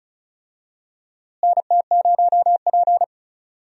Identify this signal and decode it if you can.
Morse code, heard as a keyed tone, about 36 wpm, 700 Hz. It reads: NT0P
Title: USA Call Signs - 4 Characters - Repeated 3x - 35wpm